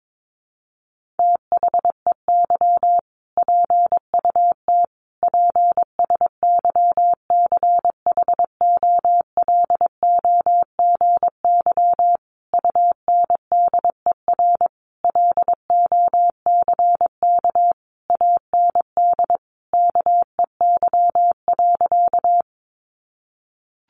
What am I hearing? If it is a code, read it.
Morse code, heard as a keyed tone, about 22 wpm, 700 Hz. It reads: THEY PUT PSYCHOLOGY UNDER LOCK AND KEY.
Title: Sentences from Top 1400 Words 22wpm